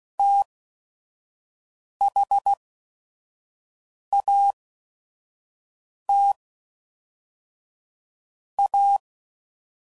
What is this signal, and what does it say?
Morse code, heard as a keyed tone, about 16 wpm, spaced as for 5 wpm, 800 Hz. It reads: THAT A
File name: CW Code Groups @5WPM / Technical Subjects / SwPwrSup